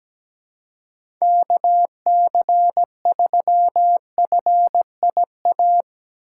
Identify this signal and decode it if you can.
Morse code, heard as a keyed tone, about 17 wpm, 700 Hz. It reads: KC3FIA
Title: USA Call Signs - 6 Characters - 4x Character Spacing - 17wpm